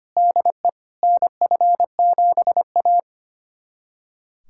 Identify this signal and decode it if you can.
Morse code, heard as a keyed tone, about 25 wpm, 700 Hz. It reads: DE NF7A
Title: USA Callsign DE Callsign - 4 Characters - 25wpm